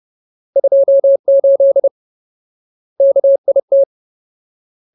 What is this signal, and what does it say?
Morse code, heard as a keyed tone, about 30 wpm, 550 Hz. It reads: 28 KIT
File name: Ultimate Word List / 30WPM Ultimate